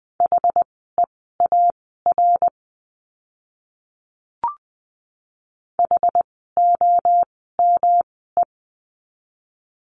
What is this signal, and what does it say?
Morse code, heard as a keyed tone, about 20 wpm, spaced as for 14 wpm, 700 Hz. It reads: HEAR HOME
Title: 108 Lesson 18f - Letter m - Words 20wpm